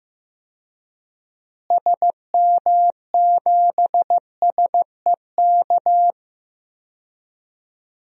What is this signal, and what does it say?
Morse code, heard as a keyed tone, about 15 wpm, 700 Hz. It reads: SM7SEK
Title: DXCC-100 Call Signs - 6 Characters - Repeated 3x - 15wpm